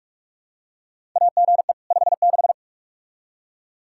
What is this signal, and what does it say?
Morse code, heard as a keyed tone, about 44 wpm, 700 Hz. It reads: AGE 56
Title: K1USN SST QSO Practice 45wpm